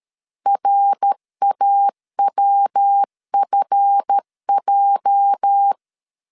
Morse code, heard as a keyed tone, about 13 wpm, 800 Hz. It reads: RAWFJ
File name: CW / DET / 1 9 1500C S C 12wpm 3811